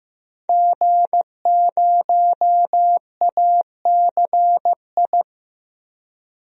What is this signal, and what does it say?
Morse code, heard as a keyed tone, about 15 wpm, 700 Hz. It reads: G0ACI
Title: DXCC-100 Call Signs - 5 Characters - 15wpm